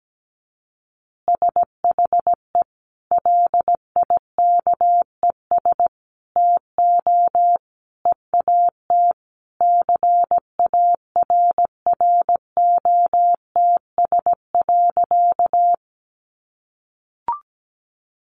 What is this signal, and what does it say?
Morse code, heard as a keyed tone, about 17 wpm, 700 Hz. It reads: SHE LIKES TO EAT CARROTS.
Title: Easy Phrases 17wpm